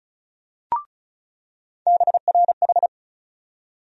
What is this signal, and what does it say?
Morse code, heard as a keyed tone, about 36 wpm, 700 Hz. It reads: BRH
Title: USA Call Sign Postfixes - 3 Letters 35wpm